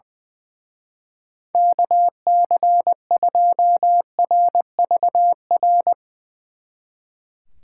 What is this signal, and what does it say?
Morse code, heard as a keyed tone, about 21 wpm, 700 Hz. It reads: KC2RVR
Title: Speed Racing - USA Call Signs 20wpm